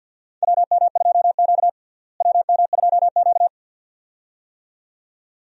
Morse code, heard as a keyed tone, about 52 wpm, 700 Hz. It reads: WM2Q WM2Q
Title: USA Call Signs - 4 Characters - Repeated 2x - 50wpm